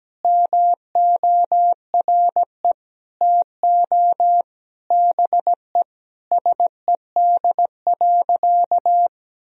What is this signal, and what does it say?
Morse code, heard as a keyed tone, about 17 wpm, 700 Hz. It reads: MORE TO BE SED.
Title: Sentences from Top 500 Words and Abbreviations 17wpm